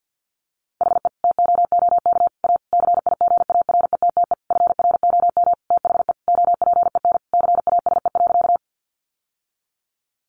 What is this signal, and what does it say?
Morse code, heard as a keyed tone, about 50 wpm, 700 Hz. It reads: HE TOOK A CIGARETTE FROM THE OPEN CASE.
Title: Sentences from Top 1000 Words 50wpm